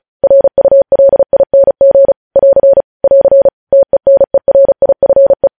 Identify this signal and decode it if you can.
Morse code, heard as a keyed tone, about 36 wpm, 550 Hz. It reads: RULING <AR> <AR> TENERIFE
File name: podcast / News headlines-1x-35x35